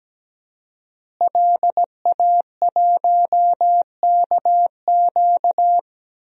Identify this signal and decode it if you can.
Morse code, heard as a keyed tone, about 17 wpm, 700 Hz. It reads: LA1KQ
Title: DXCC-100 Call Signs - 5 Characters - Repeated 3x - 17wpm